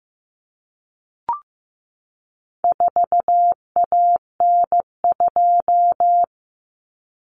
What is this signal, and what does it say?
Morse code, heard as a keyed tone, about 15 wpm, 700 Hz. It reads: 4AN2
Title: Common Cut Numbers - 4-Digit 15wpm